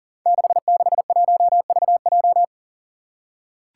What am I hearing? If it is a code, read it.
Morse code, heard as a keyed tone, about 41 wpm, 700 Hz. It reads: 661VJ
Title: Five Challenging Letter-Number Combinations 40wpm